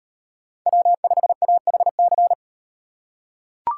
Morse code, heard as a keyed tone, about 38 wpm, 700 Hz. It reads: W5AHC